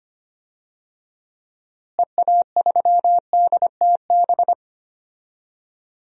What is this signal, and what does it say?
Morse code, heard as a keyed tone, about 25 wpm, 700 Hz. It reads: EA3DTB
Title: DXCC-100 Call Signs - 6 Characters - Repeated 3x - 25wpm